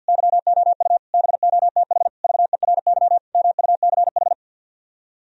Morse code, heard as a keyed tone, about 65 wpm, spaced as for 57 wpm, 700 Hz. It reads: YOU BOTH VERY MUCH
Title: Speed-Racing - Sets of 5 Words 50wpm